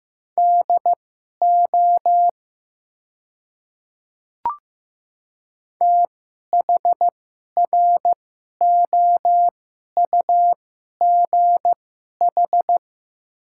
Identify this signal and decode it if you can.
Morse code, heard as a keyed tone, about 16 wpm, spaced as for 11 wpm, 700 Hz. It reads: DO THROUGH
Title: 2x Character Spacing - Top 200 Words - 15wpm